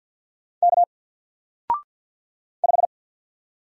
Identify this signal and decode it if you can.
Morse code, heard as a keyed tone, about 46 wpm, 700 Hz. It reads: K 5
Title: Single Letter-Number 50wpm